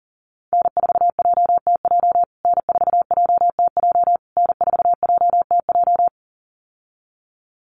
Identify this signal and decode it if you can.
Morse code, heard as a keyed tone, about 37 wpm, 700 Hz. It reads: N4JTJ N4JTJ N4JTJ
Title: USA Call Signs Repeated 3x 40wpm